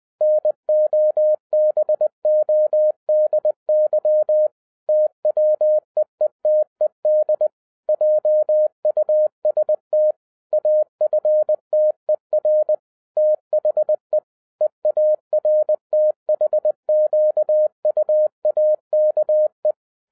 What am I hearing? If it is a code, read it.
Morse code, heard as a keyed tone, about 20 wpm, 610 Hz. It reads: NOBODY TWEETED JUST AFTER THE EARTHQUAKE